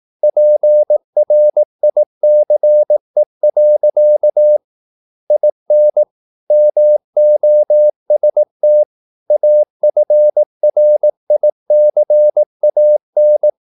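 Morse code, heard as a keyed tone, about 18 wpm, 600 Hz. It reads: PRICE. IN MOST AFRICAN
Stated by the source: Morse practice at 18 wpm, English text, from CW med Gnister.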